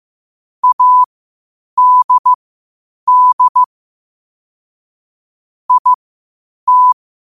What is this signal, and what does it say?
Morse code, heard as a keyed tone, about 15 wpm, spaced as for 8 wpm, 1000 Hz. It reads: ADD IT
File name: qotdfiles / 8wpm